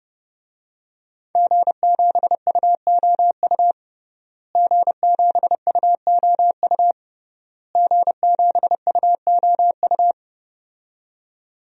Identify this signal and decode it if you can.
Morse code, heard as a keyed tone, about 32 wpm, 700 Hz. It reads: G7UOU G7UOU G7UOU
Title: DXCC-100 Call Signs - 5 Characters - Repeated 3x - 30wpm